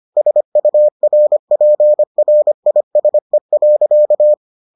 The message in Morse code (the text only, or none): SURPRISE.